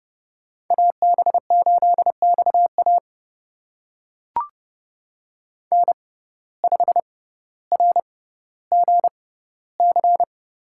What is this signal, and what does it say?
Morse code, heard as a keyed tone, about 30 wpm, spaced as for 15 wpm, 700 Hz. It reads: AB8XA N5RGC
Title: USA Call Signs - 5 Characters - 6x Character Spacing - 30wpm